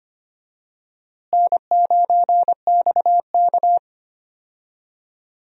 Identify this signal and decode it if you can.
Morse code, heard as a keyed tone, about 25 wpm, 700 Hz. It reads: N9XK
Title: USA Call Signs - 4 Characters - Repeated 2x - 25wpm